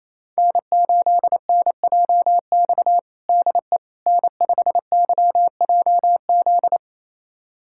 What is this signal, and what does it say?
Morse code, heard as a keyed tone, about 28 wpm, 700 Hz. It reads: N8NJX DE N5YJZ
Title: USA Callsign DE Callsign - 5 Characters - 28wpm